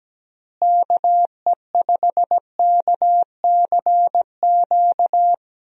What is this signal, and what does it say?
Morse code, heard as a keyed tone, about 17 wpm, 700 Hz. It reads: KE5KCQ
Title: USA Call Signs - 6 Characters - Repeated 2x - 17wpm